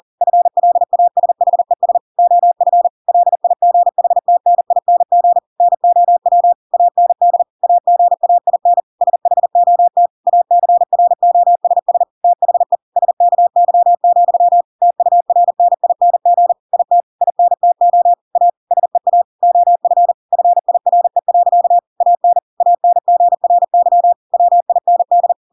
Morse code, heard as a keyed tone, about 40 wpm, 700 Hz. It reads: FLASHES OF LIGHTNING NOW AND AGAIN SHOT ACROSS THE SKY, TURNING IT INTO A SEA OF FIRE. AN ANGRY WIND